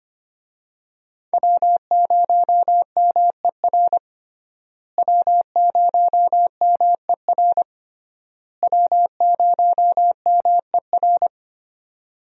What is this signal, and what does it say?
Morse code, heard as a keyed tone, about 25 wpm, 700 Hz. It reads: W0MER W0MER W0MER
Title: USA Call Signs - 5 Characters - Repeated 3x - 25wpm